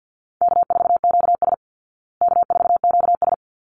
Morse code, heard as a keyed tone, about 48 wpm, 700 Hz. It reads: K4QS K4QS
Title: Easy Call Signs Repeated 2x - 50wpm